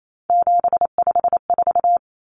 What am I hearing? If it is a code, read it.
Morse code, heard as a keyed tone, about 28 wpm, 700 Hz. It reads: 754
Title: Common Cut Numbers - 3-Digit 28wpm